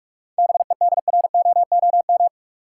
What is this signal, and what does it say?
Morse code, heard as a keyed tone, about 46 wpm, 700 Hz. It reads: BEDROOM
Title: Seven Letter Words - 45wpm